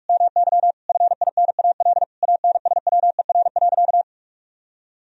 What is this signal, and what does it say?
Morse code, heard as a keyed tone, about 41 wpm, 700 Hz. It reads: MY FINAL ANSWER.
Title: Sentences from Top 1000 Words - Review Part 2 - 45wpm